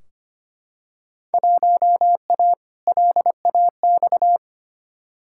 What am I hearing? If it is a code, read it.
Morse code, heard as a keyed tone, about 25 wpm, 700 Hz. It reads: 1A LAX
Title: Field Day Exchanges - 25wpm